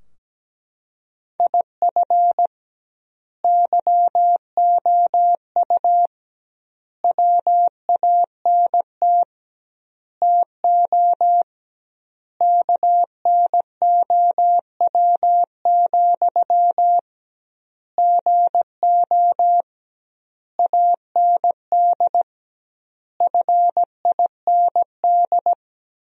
Morse code, heard as a keyed tone, about 17 wpm, 700 Hz. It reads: IF YOU WANT TO KNOW, GO AND FIND
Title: 2x Word-Spacing - Sentences from Top 200 Words - 17wpm